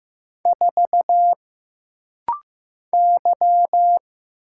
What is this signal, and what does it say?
Morse code, heard as a keyed tone, about 15 wpm, 700 Hz. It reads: 4 Y
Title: Single Letter-Number Alt - Rapid-Fire 15wpm